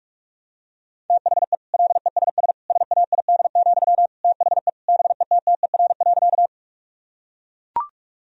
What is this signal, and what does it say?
Morse code, heard as a keyed tone, about 41 wpm, 700 Hz. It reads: THE LESS SAID, THE BETTER.
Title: Sentences from Top 1000 Words - Review Part 2 - 45wpm